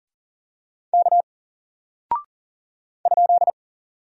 Morse code, heard as a keyed tone, about 40 wpm, 700 Hz. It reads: K ?
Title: Single Character (Letter:Number:Punctuation:Prosign) Alt 40wpm